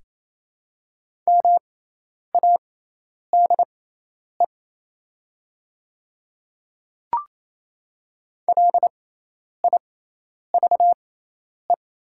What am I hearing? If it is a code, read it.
Morse code, heard as a keyed tone, about 28 wpm, spaced as for 10 wpm, 700 Hz. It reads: MADE LIVE
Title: 6x Character Spacing - Top 200 Words - 28wpm